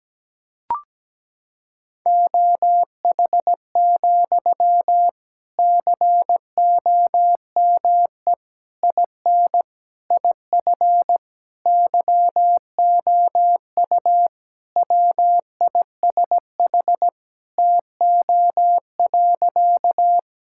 Morse code, heard as a keyed tone, about 17 wpm, 700 Hz. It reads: OH, COME IN IF YOU WISH TO.
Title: Sentences from Top 1600 Words 17wpm